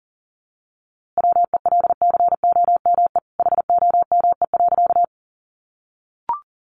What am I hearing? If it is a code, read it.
Morse code, heard as a keyed tone, about 40 wpm, 700 Hz. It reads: WELCOME HOME.
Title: A Sound Of Thunder 40wpm